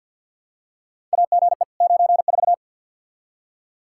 Morse code, heard as a keyed tone, about 51 wpm, 700 Hz. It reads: AGE 94
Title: Practice QSO - Exchange 3 - More Info - 50wpm